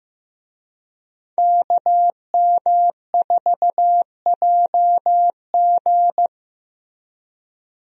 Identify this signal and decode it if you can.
Morse code, heard as a keyed tone, about 15 wpm, 700 Hz. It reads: KM4JG
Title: USA Call Signs - 5 Characters - Repeated 2x - 15wpm